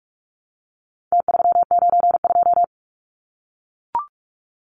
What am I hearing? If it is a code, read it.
Morse code, heard as a keyed tone, about 44 wpm, 700 Hz. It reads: T392
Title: Numbers - Contest Style - 4-Digit 45wpm